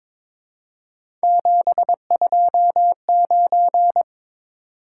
Morse code, heard as a keyed tone, about 22 wpm, 700 Hz. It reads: 729
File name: practice / 22wpm / Three Letter-Number Combinations - 22wpm